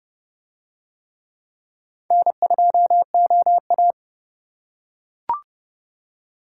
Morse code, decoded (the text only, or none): N2OA